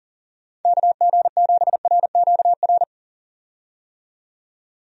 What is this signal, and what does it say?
Morse code, heard as a keyed tone, about 40 wpm, 690 Hz. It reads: KG7RQR
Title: USA Call Signs - 6 Characters - 40wpm